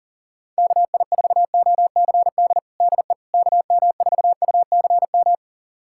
Morse code, heard as a keyed tone, about 40 wpm, 700 Hz. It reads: KI4OCD DE KM4UCM
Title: USA Callsign DE Callsign - 6 Characters - 40wpm